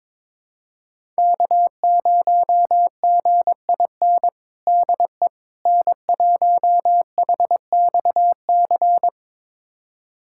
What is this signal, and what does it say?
Morse code, heard as a keyed tone, about 21 wpm, 700 Hz. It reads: K0GIN DE N1HXC
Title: USA Callsign DE Callsign - 5 Characters - 22wpm